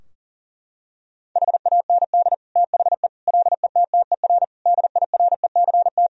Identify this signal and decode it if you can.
Morse code, heard as a keyed tone, about 40 wpm, 700 Hz. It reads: HAND THE LETTER DIRECT
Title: Sentences from Top 1000 Words - Review Part 2 - 40wpm